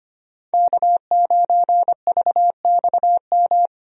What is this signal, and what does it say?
Morse code, heard as a keyed tone, about 25 wpm, 700 Hz. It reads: K9VXM
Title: USA Call Signs - 5 Characters - 25wpm